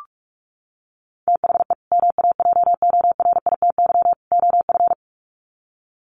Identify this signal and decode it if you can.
Morse code, heard as a keyed tone, about 44 wpm, 700 Hz. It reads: THE MAJORITY OF
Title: Sets of 3 Words 45wpm